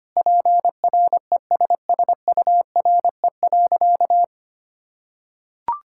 Morse code, heard as a keyed tone, about 25 wpm, 700 Hz. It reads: PRESSURE.
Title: Sentences from Top 1000 Words - Review Part 6 - 25wpm